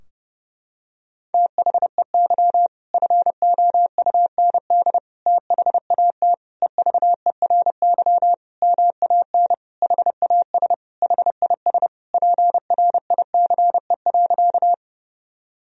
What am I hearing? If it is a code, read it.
Morse code, heard as a keyed tone, about 32 wpm, 700 Hz. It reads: THEY FOUND THAT EVERY MAN HAS HIS PRICE.
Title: Sentences from Top 1000 Words - Review Part 4 - 30wpm